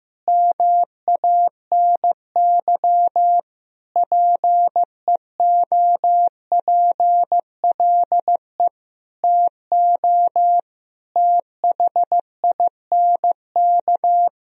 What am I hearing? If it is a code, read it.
Morse code, heard as a keyed tone, about 15 wpm, 700 Hz. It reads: MANY PEOPLE TO THINK
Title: Sentences from Top 1000 Words - Review Part 1 - 15wpm